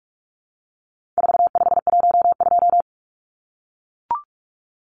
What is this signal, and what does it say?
Morse code, heard as a keyed tone, about 44 wpm, 700 Hz. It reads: 4512
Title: Numbers - 4-Digit 45wpm